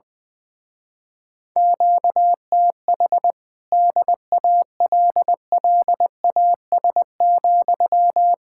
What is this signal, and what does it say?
Morse code, heard as a keyed tone, about 20 wpm, 700 Hz. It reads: QTH DALLAS,
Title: Speed-Racing Common QSO Phrases - 20wpm